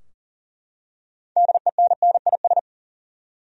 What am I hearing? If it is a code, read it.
Morse code, heard as a keyed tone, about 40 wpm, 700 Hz. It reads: DENNIS
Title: USA Common Names - 40wpm